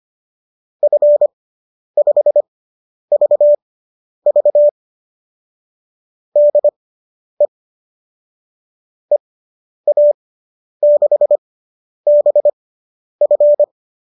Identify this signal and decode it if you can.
Morse code, heard as a keyed tone, about 25 wpm, spaced as for 10 wpm, 590 Hz. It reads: F5VV DE EA6BF